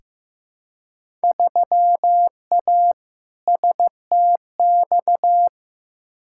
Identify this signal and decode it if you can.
Morse code, heard as a keyed tone, about 15 wpm, 700 Hz. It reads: 3A STX
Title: Field Day Exchanges - 15wpm